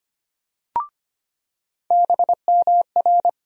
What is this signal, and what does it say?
Morse code, heard as a keyed tone, about 25 wpm, 700 Hz. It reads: BMR
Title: USA Call Sign Postfixes - 3 Letters 25wpm